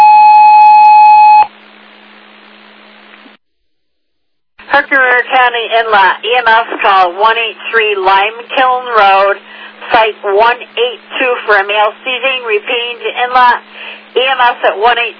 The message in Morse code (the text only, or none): none